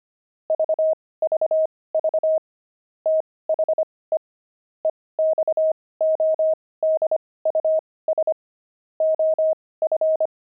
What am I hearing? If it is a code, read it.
Morse code, heard as a keyed tone, about 25 wpm, spaced as for 18 wpm, 630 Hz. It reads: VVV THE EXODUS OF
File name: pangram-fs-18-cs-25-pigeons